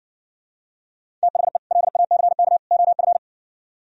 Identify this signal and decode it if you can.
Morse code, heard as a keyed tone, about 68 wpm, 700 Hz. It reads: THE LACK OF